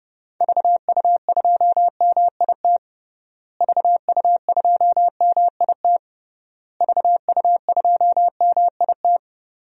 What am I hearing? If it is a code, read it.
Morse code, heard as a keyed tone, about 30 wpm, 700 Hz. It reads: VU2MIT VU2MIT VU2MIT
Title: DXCC-100 Call Signs - 6 Characters - Repeated 3x - 30wpm